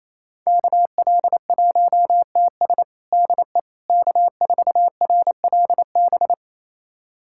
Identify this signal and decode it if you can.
Morse code, heard as a keyed tone, about 27 wpm, 700 Hz. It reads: KL1TS DE K4RLB
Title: USA Callsign DE Callsign 28wpm